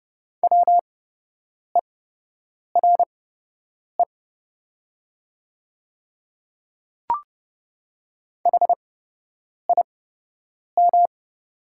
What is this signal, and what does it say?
Morse code, heard as a keyed tone, about 30 wpm, spaced as for 8 wpm, 700 Hz. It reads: WERE HIM
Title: 8x Character Spacing - Top 500 Words - 30wpm